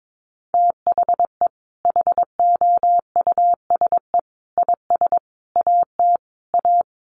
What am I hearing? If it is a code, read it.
Morse code, heard as a keyed tone, about 22 wpm, 700 Hz. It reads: THE HOUSE IS AT A
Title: Sentences from Top 1500 Words 22wpm